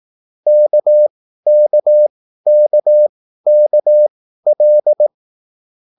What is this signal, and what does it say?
Morse code, heard as a keyed tone, about 18 wpm, spaced as for 13 wpm, 600 Hz. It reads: KKKKL
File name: QSL / CW lesson